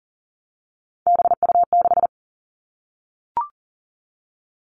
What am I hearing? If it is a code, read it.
Morse code, heard as a keyed tone, about 42 wpm, 700 Hz. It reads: BU6
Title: Three Challenging Letter-Number Combinations 40wpm